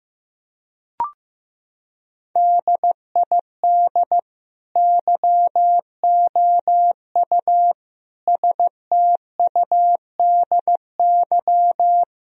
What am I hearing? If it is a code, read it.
Morse code, heard as a keyed tone, about 16 wpm, 700 Hz. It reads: DID YOU STUDY
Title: Sentences from Top 1000 Words - Review Part 3 - 15wpm